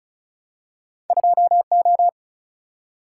Morse code, heard as a keyed tone, about 36 wpm, 700 Hz. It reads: 2O
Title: Single-Letter Single-Number 35wpm